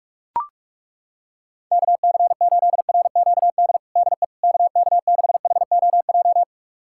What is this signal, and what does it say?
Morse code, heard as a keyed tone, about 41 wpm, 700 Hz. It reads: KC8RQD DE KK6HOJ
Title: USA Callsign DE Callsign - 6 Characters - 45wpm